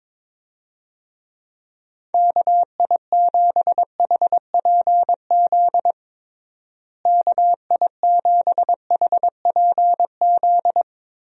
Morse code, heard as a keyed tone, about 22 wpm, 700 Hz. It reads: KI7HPZ KI7HPZ
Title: USA Call Signs - 6 Characters - Repeated 2x - 22wpm